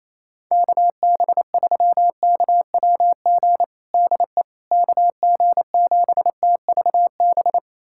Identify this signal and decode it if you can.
Morse code, heard as a keyed tone, about 28 wpm, 700 Hz. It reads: KB3KWG DE KG7TVB
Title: USA Callsign DE Callsign 28wpm